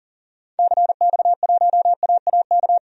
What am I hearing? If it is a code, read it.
Morse code, heard as a keyed tone, about 40 wpm, 700 Hz. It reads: CX1AAK
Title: DXCC-100 Call Signs - 6 Characters - Repeated 2x - 40wpm